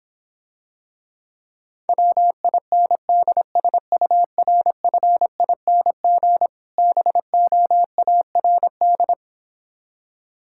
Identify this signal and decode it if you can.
Morse code, heard as a keyed tone, about 26 wpm, 700 Hz. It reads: WINDSURFING BOARD